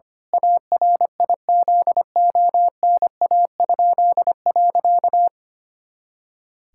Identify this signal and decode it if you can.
Morse code, heard as a keyed tone, about 25 wpm, 700 Hz. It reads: ARIZONA?.
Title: Pangram Challenge v2 - 25wpm